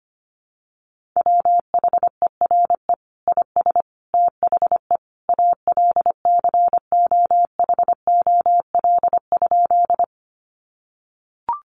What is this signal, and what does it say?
Morse code, heard as a keyed tone, about 25 wpm, 700 Hz. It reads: WHERE IS THE ALCOHOL?
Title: Sentences from Top 1400 Words 25wpm